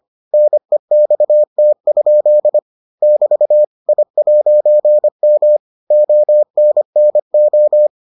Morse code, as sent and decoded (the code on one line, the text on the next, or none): -. -..- - ..--.. / -...- / .. .----. -- / --- -. -. ---
NEXT? <BT> I'M ONNO